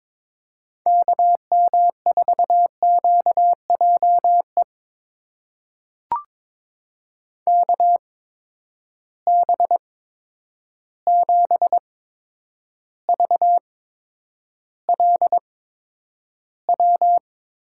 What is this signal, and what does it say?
Morse code, heard as a keyed tone, about 22 wpm, spaced as for 9 wpm, 700 Hz. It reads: KM4QJE KB7VLW
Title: USA Call Signs - 6 Characters - 8x Character Spacing - 22wpm